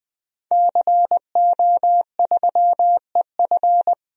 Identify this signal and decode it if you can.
Morse code, heard as a keyed tone, about 20 wpm, 700 Hz. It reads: CO3EF
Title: DXCC-100 Call Signs - 5 Characters - 20wpm